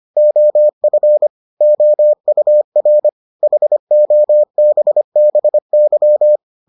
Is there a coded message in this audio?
Yes: Morse code from 0.2 to 6.4 s, heard as a keyed tone, about 25 wpm, 600 Hz. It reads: OF OUR HOBBY